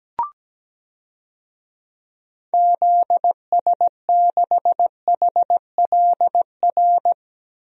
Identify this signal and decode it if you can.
Morse code, heard as a keyed tone, about 17 wpm, 700 Hz. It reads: ZS6HLR
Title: DXCC-100 Call Signs - 6 Characters - Repeated 3x - 17wpm